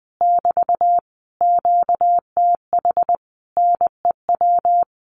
Morse code, heard as a keyed tone, about 20 wpm, 700 Hz. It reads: <BT> QTH NEW
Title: Common QSO Phrases Encore 20wpm